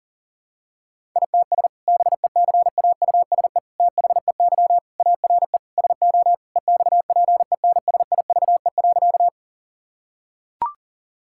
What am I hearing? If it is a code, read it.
Morse code, heard as a keyed tone, about 40 wpm, 700 Hz. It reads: ITS BECAUSE THEY ARE SO EXPENSIVE.
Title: Sentences from Top 1000 Words - Review Part 6 - 40wpm